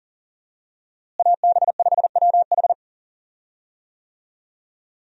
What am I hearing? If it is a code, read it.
Morse code, heard as a keyed tone, about 40 wpm, 700 Hz. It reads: AB5WH
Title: USA Call Signs - 5 Characters - Repeated 2x - 40wpm